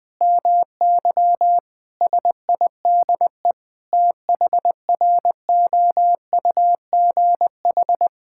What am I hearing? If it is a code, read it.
Morse code, heard as a keyed tone, about 20 wpm, 700 Hz. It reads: MY SIDE THROUGH
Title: Sentences from Top 1600 Words 20wpm